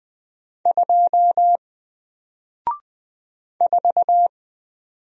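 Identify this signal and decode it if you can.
Morse code, heard as a keyed tone, about 20 wpm, 700 Hz. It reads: 2 4
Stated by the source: Single letter and number Morse drill at 20 wpm, from N4LCD.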